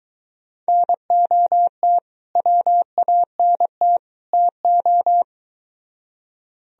Morse code, heard as a keyed tone, about 22 wpm, 700 Hz. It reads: NOT WANT TO